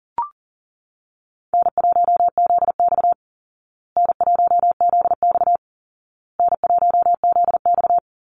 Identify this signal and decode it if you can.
Morse code, heard as a keyed tone, about 40 wpm, 700 Hz. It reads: N1ZX N1ZX N1ZX
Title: Easy Call Signs Repeated 3x - 40wpm